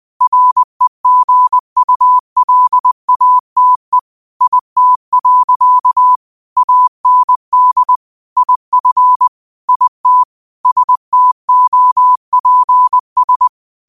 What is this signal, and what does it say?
Morse code, heard as a keyed tone, about 20 wpm, 1000 Hz. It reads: REGULATE IT. AND IF IT STOPS